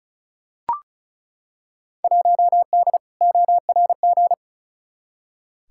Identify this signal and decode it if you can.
Morse code, heard as a keyed tone, about 36 wpm, 700 Hz. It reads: 1D ORG
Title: Field Day Exchanges - 35wpm